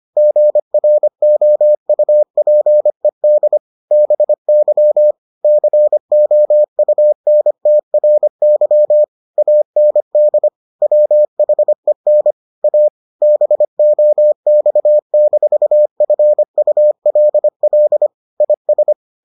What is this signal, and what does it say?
Morse code, heard as a keyed tone, about 25 wpm, 600 Hz. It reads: GROUPED BY COUNTRY AND WHEN A BOX-FULL IS